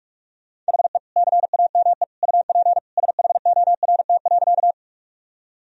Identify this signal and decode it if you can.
Morse code, heard as a keyed tone, about 46 wpm, 700 Hz. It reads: HE CAME UP SHORT.
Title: Sentences from Top 1000 Words - Review Part 2 - 45wpm